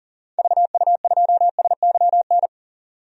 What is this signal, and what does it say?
Morse code, heard as a keyed tone, about 40 wpm, 700 Hz. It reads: VU2SYN
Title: DXCC-100 Call Signs - 6 Characters - 40wpm